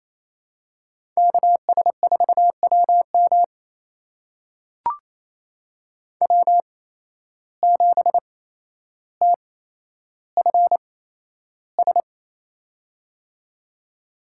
Morse code, heard as a keyed tone, about 28 wpm, spaced as for 12 wpm, 700 Hz. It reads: KS4WM W7TFS
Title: USA Call Signs - 5 Characters - 8x Character Spacing - 28wpm